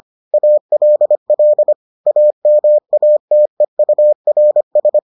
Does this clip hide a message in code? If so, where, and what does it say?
Yes: Morse code from 0.3 to 5.0 s, heard as a keyed tone, about 25 wpm, 600 Hz. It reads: ALL AMATEURS